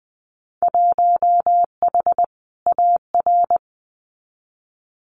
Morse code, heard as a keyed tone, about 20 wpm, 700 Hz. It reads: 1H AR
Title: Winter Field Day Exchanges 20wpm